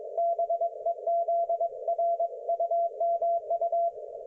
Morse code, heard as a keyed tone, about 21 wpm, 680 Hz, with radio noise. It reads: BEZ RUMU
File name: Archiv / CUC216 / ok1fhi cul216